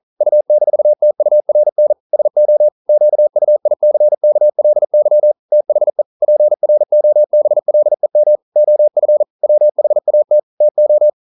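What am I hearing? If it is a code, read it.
Morse code, heard as a keyed tone, about 42 wpm, 600 Hz. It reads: U-TURN SO QUICKLY THE PROBLEM OF WHAT TO
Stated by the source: Morse code practice at 41 wpm, from CW med Gnister.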